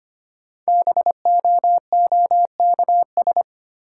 Morse code, heard as a keyed tone, about 25 wpm, 700 Hz. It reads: BOOKS
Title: Sentences from Top 1000 Words - Review Part 4 - New Word Drill - 25wpm